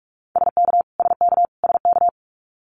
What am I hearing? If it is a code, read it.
Morse code, heard as a keyed tone, about 46 wpm, 700 Hz. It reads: SK SK SK
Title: Canadian Provinces and Territories - Repeated 3x - 45wpm